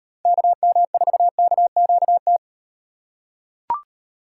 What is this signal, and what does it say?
Morse code, heard as a keyed tone, about 38 wpm, 700 Hz. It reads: KM4KQT